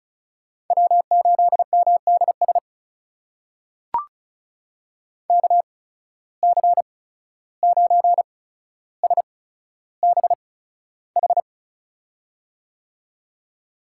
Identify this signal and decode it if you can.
Morse code, heard as a keyed tone, about 36 wpm, spaced as for 14 wpm, 700 Hz. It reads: W8MDS KC9SBH
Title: USA Call Signs - 8x Character Spacing - 35wpm